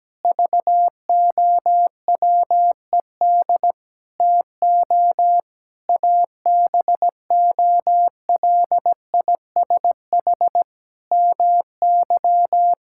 Morse code, heard as a keyed tone, about 17 wpm, 700 Hz. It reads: VOWED TO ABOLISH MY
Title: Pangram Challenge v2 - 17wpm